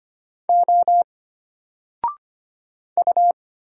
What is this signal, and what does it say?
Morse code, heard as a keyed tone, about 25 wpm, 700 Hz. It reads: O U
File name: Single-Letter-Number-25wpm